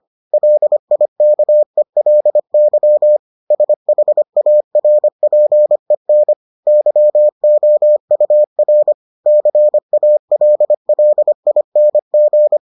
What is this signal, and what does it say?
Morse code, heard as a keyed tone, about 25 wpm, 600 Hz. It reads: LIKELY SHARPEN YOUR CALLING